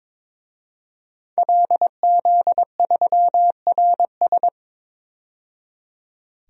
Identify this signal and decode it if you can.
Morse code, heard as a keyed tone, about 22 wpm, 700 Hz. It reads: LZ3RS
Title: DXCC-100 Call Signs - 5 Characters - 22wpm